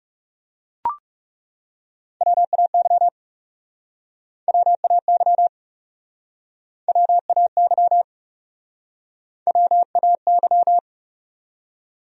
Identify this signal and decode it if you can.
Morse code, heard as a keyed tone, about 36 wpm, 700 Hz. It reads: WAY WAY WAY WAY
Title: Speed-Racing Top 100 Words 30wpm